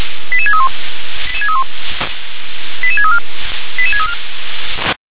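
Five-tone selective call (ZVEI), heard as quick runs of five tones, tones of about 70 ms. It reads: 88631 88634 88634
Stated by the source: ZVEI five-tone selective call received on SDR.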